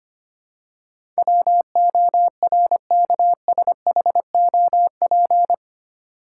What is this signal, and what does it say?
Morse code, heard as a keyed tone, about 25 wpm, 700 Hz. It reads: WORKSHOP
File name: 8 Letter Words - First 3 Letters Given - 25wpm